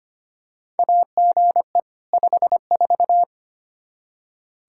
Morse code, heard as a keyed tone, about 25 wpm, 700 Hz. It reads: AGE 54
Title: K1USN SST QSO Practice 25wpm